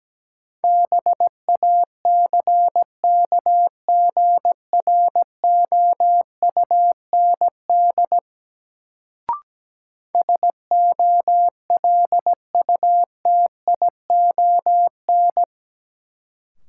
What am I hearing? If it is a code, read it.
Morse code, heard as a keyed tone, about 17 wpm, 700 Hz. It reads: BACKGROUND SOLUTION
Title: Sentences from Top 1300 Words - New Word Drill - 17wpm